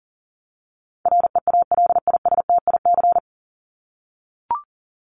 Morse code, heard as a keyed tone, about 40 wpm, 700 Hz. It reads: REALISTIC
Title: Top 1751-2000 Words 40wpm